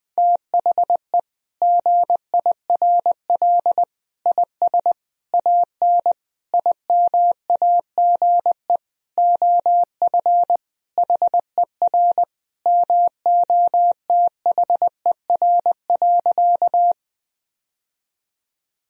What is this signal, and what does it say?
Morse code, heard as a keyed tone, about 20 wpm, 700 Hz. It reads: THE GIRL IS AN IMAGE OF HER MOTHER.
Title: Sentences from Top 1000 Words - Review Part 4 - 20wpm